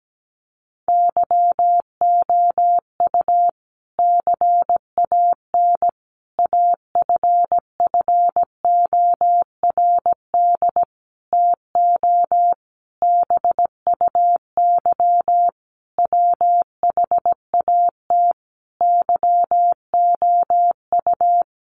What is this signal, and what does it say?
Morse code, heard as a keyed tone, about 17 wpm, 700 Hz. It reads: YOU CAN AFFORD TO BUY WHAT YOU
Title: Sentences from Top 1100 Words 17wpm